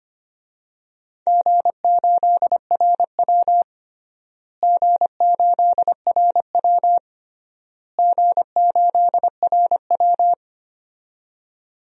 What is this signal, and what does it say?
Morse code, heard as a keyed tone, about 25 wpm, 700 Hz. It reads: G8RW G8RW G8RW
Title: DXCC-100 Call Signs - 4 Characters - Repeated 3x - 25wpm